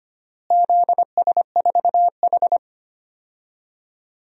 Morse code, heard as a keyed tone, about 25 wpm, 700 Hz. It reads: ZS4H
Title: DXCC-100 Call Signs - 4 Characters - Repeated 3x - 25wpm